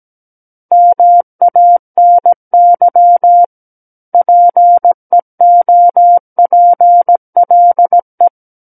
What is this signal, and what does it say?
Morse code, heard as a keyed tone, about 17 wpm, 700 Hz. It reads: MANY PEOPLE
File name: Book One / 08-Chapter 8 - 17 WPM